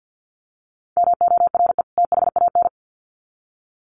Morse code, heard as a keyed tone, about 50 wpm, 700 Hz. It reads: MORE THAN
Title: Sets of 2 Words 50wpm